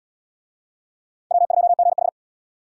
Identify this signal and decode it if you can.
Morse code, heard as a keyed tone, about 78 wpm, 700 Hz. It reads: W2GH